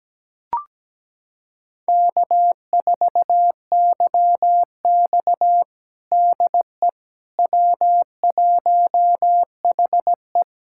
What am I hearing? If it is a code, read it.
Morse code, heard as a keyed tone, about 17 wpm, 700 Hz. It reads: K4YX DE W1HE
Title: USA Callsign DE Callsign - 4 Characters - 17wpm